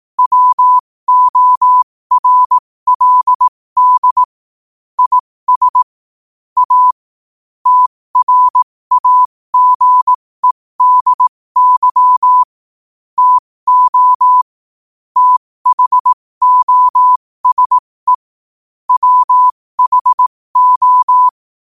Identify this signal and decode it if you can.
Morse code, heard as a keyed tone, about 18 wpm, spaced as for 15 wpm, 1000 Hz. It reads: WORLD IS A TRAGEDY TO THOSE WHO
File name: qotdfiles / 15wpm